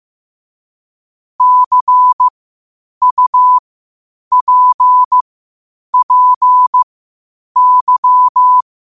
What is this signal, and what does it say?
Morse code, heard as a keyed tone, about 15 wpm, spaced as for 8 wpm, 1000 Hz. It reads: CUPPY
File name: qotdfiles / 8wpm